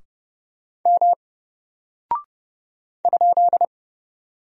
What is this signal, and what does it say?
Morse code, heard as a keyed tone, about 30 wpm, 700 Hz. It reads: M ?
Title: Single Character (Letter:Number:Punctuation:Prosign) 30wpm